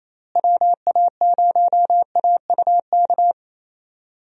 Morse code, heard as a keyed tone, about 28 wpm, 700 Hz. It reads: WA0AUK
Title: USA Call Signs - 6 Characters - Repeated 2x - 28wpm